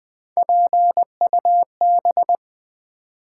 Morse code, heard as a keyed tone, about 20 wpm, 700 Hz. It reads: PUB
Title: Build A Word - 3 Letters At A Time - 9-Letter Words - 20wpm